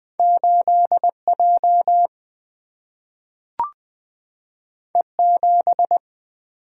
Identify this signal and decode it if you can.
Morse code, heard as a keyed tone, about 20 wpm, 700 Hz. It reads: 8 J E 7
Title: Single-Letter Single-Number 20wpm